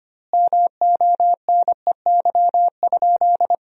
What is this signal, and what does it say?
Morse code, heard as a keyed tone, about 25 wpm, 700 Hz. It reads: MONEY?
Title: 2x Word-Spacing - Sentences from Top 400 Words - 25wpm